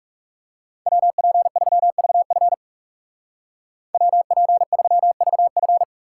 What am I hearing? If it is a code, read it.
Morse code, heard as a keyed tone, about 35 wpm, 700 Hz. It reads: WP3VF WP3VF
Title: Speed Racing - USA 2x2 Call Signs 30wpm